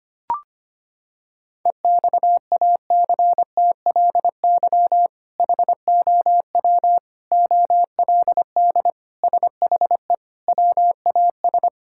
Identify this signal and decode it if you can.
Morse code, heard as a keyed tone, about 25 wpm, 700 Hz. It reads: EXACTLY HOW OLD SHE WAS
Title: Sentences from Top 1600 Words 25wpm